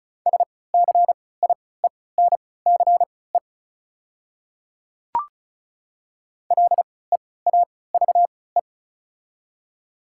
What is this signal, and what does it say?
Morse code, heard as a keyed tone, about 36 wpm, spaced as for 20 wpm, 690 Hz. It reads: SCIENCE LEAVE
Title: 3x Character Spacing - Top 500 Words - 35wpm